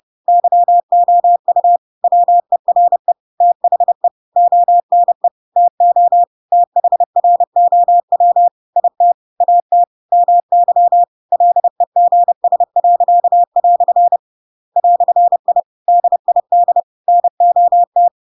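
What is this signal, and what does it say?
Morse code, heard as a keyed tone, about 30 wpm, 700 Hz. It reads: YOU WERE THE ONE TO THROW IT AT MY LEGS." "I DID NOT
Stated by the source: Chapter 2 of The Adventures of Pinocchio read as a book in Morse code (CW) at 30 wpm, by ik1hge.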